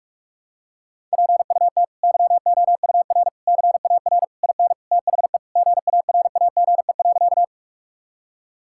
Morse code, heard as a keyed tone, about 43 wpm, 700 Hz. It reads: PUT YOUR CAR IN THE GARAGE.
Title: Sentences from Top 1600 Words 45wpm